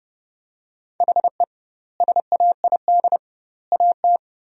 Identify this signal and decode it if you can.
Morse code, heard as a keyed tone, about 30 wpm, 700 Hz. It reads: HE SAID AT
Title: 2x Word-Spacing - Sentences from Top 300 Words - 30wpm